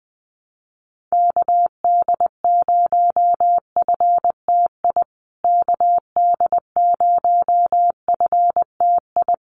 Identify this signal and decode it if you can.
Morse code, heard as a keyed tone, about 20 wpm, 700 Hz. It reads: KD0FTI KD0FTI
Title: USA Call Signs Repeated 3x 20wpm